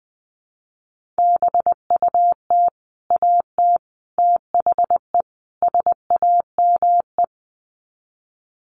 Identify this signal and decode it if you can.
Morse code, heard as a keyed tone, about 20 wpm, 700 Hz. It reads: BUT AT THE SAME
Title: Sets of 4 Words 20wpm